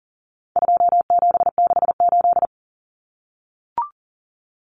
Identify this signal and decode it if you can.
Morse code, heard as a keyed tone, about 40 wpm, 700 Hz. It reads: 2768
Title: Common Cut Numbers - 4-Digit 40wpm